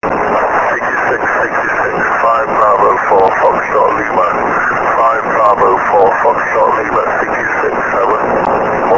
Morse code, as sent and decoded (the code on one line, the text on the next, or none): none